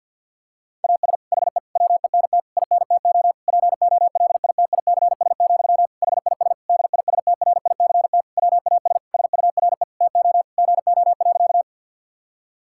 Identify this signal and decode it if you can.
Morse code, heard as a keyed tone, about 50 wpm, 700 Hz. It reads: AS HE WENT INTO POLITICS, HIS DISTRICT WAS SURE TO GO.
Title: Sentences from Top 1600 Words 50wpm